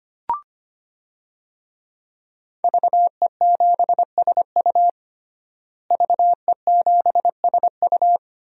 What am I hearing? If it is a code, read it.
Morse code, heard as a keyed tone, about 25 wpm, 700 Hz. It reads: VE7SU VE7SU
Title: DXCC-100 Call Signs - 5 Characters - Repeated 2x - 25wpm